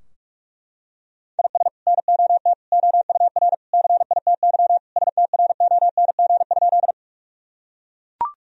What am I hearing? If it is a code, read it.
Morse code, heard as a keyed tone, about 41 wpm, 700 Hz. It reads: IS NOT OUR CITY STRONG?
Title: Sentences from Top 1000 Words - Review Part 3 - 45wpm